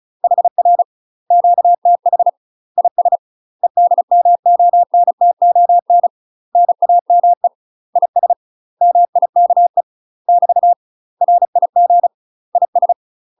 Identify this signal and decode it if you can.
Morse code, heard as a keyed tone, about 36 wpm, 700 Hz. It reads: HR QTH IS EDMONTON NAME IS MIKE <BT> RIG IS
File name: UK MASTER QSOs TRAINING / 35WPM / UK MASTER QSOs 0031